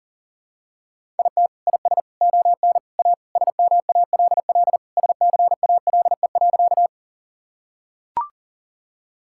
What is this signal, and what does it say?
Morse code, heard as a keyed tone, about 40 wpm, 700 Hz. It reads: IT IS ON A SMALL SCALE.
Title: Sentences from Top 1000 Words - Review Part 5 - 40wpm